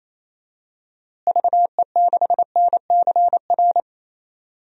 Morse code, heard as a keyed tone, about 28 wpm, 700 Hz. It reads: VE6NCR
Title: DXCC-100 Call Signs - 6 Characters - Repeated 3x - 28wpm